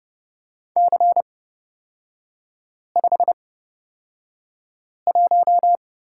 Morse code, heard as a keyed tone, about 31 wpm, spaced as for 5 wpm, 690 Hz. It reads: C51
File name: Lesson 37e - Letter x - Three Characters 30wpm